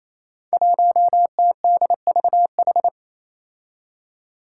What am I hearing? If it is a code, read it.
Morse code, heard as a keyed tone, about 27 wpm, 700 Hz. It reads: 1TDVH
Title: Five Challenging Letter-Number Combinations 28wpm